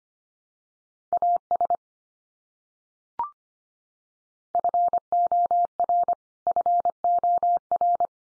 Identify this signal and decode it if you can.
Morse code, heard as a keyed tone, about 25 wpm, 700 Hz. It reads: AS FOR FOR
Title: Top 100 Words 25wpm x3